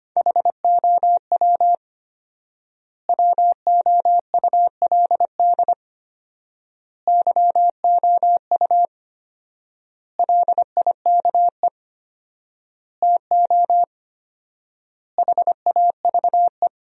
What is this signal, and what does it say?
Morse code, heard as a keyed tone, about 25 wpm, 700 Hz. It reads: HOW WOULD YOU LIKE TO HAVE
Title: 4x Word-Spacing - Sentences from Top 100 Words - 25wpm